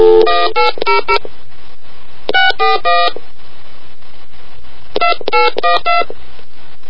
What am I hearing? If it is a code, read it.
DTMF, heard as telephone touch tones, tones of about 160 ms. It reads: 1800 641 2812